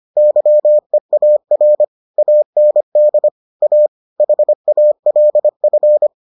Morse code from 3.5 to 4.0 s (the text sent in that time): A